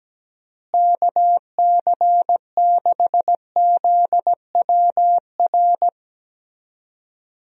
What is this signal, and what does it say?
Morse code, heard as a keyed tone, about 17 wpm, 700 Hz. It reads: KC6ZWR
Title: USA Call Signs - 6 Characters - Repeated 3x - 17wpm